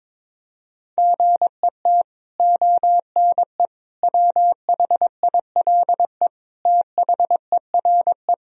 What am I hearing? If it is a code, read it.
Morse code, heard as a keyed tone, about 22 wpm, 700 Hz. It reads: GET ONE WHILE THERE
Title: Sentences from Top 1000 Words - Review Part 1 - 22wpm